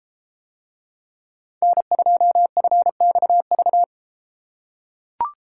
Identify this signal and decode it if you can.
Morse code, heard as a keyed tone, about 32 wpm, 700 Hz. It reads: N2FXV